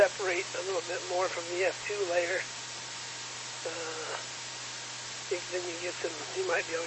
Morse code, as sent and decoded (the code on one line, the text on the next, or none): none